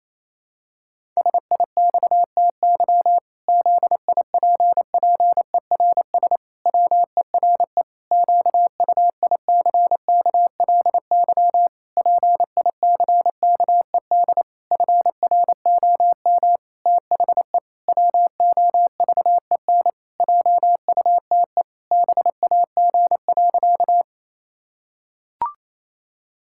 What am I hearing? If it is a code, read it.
Morse code, heard as a keyed tone, about 28 wpm, 700 Hz. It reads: SIXTY ZIPPERS WERE QUICKLY PICKED FROM THE WOVEN JUTE BAG.
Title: Pangram Challenge v2 - 28wpm